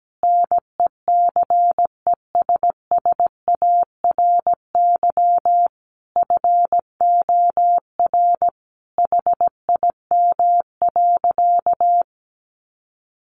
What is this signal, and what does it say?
Morse code, heard as a keyed tone, about 17 wpm, 700 Hz. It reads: NECESSARY FOR HIM.
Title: Sentences from Top 1500 Words 17wpm